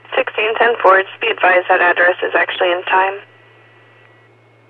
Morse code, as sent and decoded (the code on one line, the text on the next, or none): none